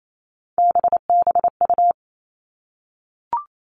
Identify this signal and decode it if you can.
Morse code, heard as a keyed tone, about 28 wpm, 700 Hz. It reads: BBU
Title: All Cut Numbers - 3-Digit 28wpm